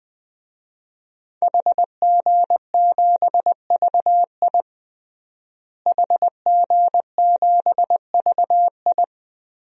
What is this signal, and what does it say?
Morse code, heard as a keyed tone, about 20 wpm, 700 Hz. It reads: HG7VI HG7VI
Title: DXCC-100 Call Signs - 5 Characters - Repeated 3x - 20wpm